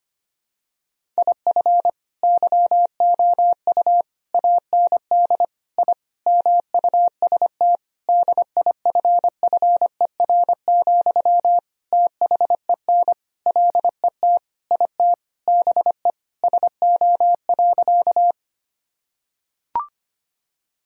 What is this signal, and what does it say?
Morse code, heard as a keyed tone, about 25 wpm, 700 Hz. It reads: IF YOU AND I MUST DIFFER, THEN LET IT BE SO.
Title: Sentences from Top 1000 Words - Review Part 1 - 25wpm